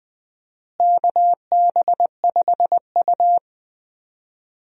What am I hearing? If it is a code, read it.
Morse code, heard as a keyed tone, about 21 wpm, 700 Hz. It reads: KB5U
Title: USA Call Signs - 4 Characters - Repeated 3x - 20wpm